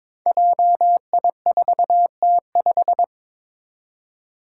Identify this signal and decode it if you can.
Morse code, heard as a keyed tone, about 22 wpm, 700 Hz. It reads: JI4T5
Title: Five Challenging Letter-Number Combinations 22wpm